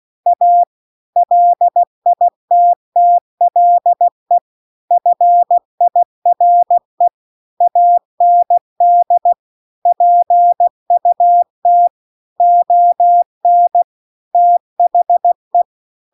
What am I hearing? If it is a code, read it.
Morse code, heard as a keyed tone, about 16 wpm, 700 Hz. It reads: A LITTLE FIRE AND PUT ON THE